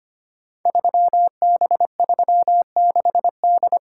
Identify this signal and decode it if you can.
Morse code, heard as a keyed tone, about 25 wpm, 700 Hz. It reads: 3B36D
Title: Code Groups - 5 Letter:Numbers - 25wpm